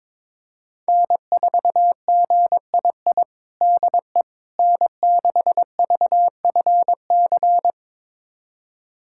Morse code, heard as a keyed tone, about 22 wpm, 700 Hz. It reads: N4GII DE N6VFC
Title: USA Callsign DE Callsign - 5 Characters - 22wpm